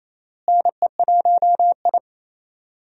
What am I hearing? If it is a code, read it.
Morse code, heard as a keyed tone, about 27 wpm, 700 Hz. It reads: NE1I
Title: USA Call Signs - 4 Characters - Repeated 2x - 28wpm